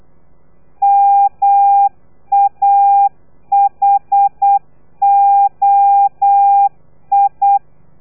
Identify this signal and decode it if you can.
Morse code, heard as a keyed tone, about 8 wpm, 800 Hz. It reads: MAHOI
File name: telegrafia / i2tfj / Lez3 35